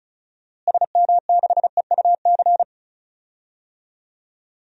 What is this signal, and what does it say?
Morse code, heard as a keyed tone, about 36 wpm, 700 Hz. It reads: SM6EUC
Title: DXCC-100 Call Signs - 6 Characters - Repeated 2x - 35wpm